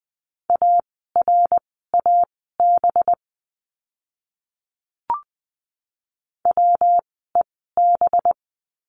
Morse code, heard as a keyed tone, about 20 wpm, spaced as for 15 wpm, 700 Hz. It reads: ARAB WEB
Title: 185 Lesson 30f - Letter b - Words 20wpm